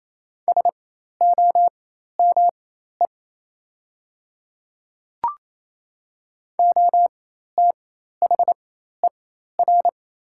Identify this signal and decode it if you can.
Morse code, heard as a keyed tone, about 28 wpm, spaced as for 13 wpm, 700 Hz. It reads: SOME OTHER
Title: 4x Character Spacing - Top 100 Words - 28wpm